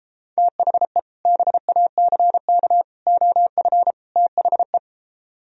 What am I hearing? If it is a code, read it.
Morse code, heard as a keyed tone, about 33 wpm, 700 Hz. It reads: THE BACK OF THE